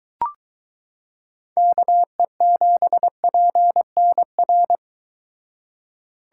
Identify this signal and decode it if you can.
Morse code, heard as a keyed tone, about 22 wpm, 700 Hz. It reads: KE7PNR